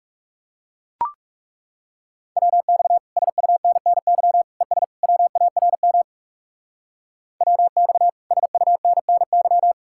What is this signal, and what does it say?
Morse code, heard as a keyed tone, about 36 wpm, 700 Hz. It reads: WX SUNNY ES WARM WX SUNNY
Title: Speed-Racing Common QSO Phrases - 30wpm